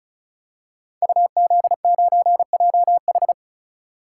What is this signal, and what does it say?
Morse code, heard as a keyed tone, about 36 wpm, 700 Hz. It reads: UZ9JH
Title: Code Groups - 5 Letter:Numbers - 35wpm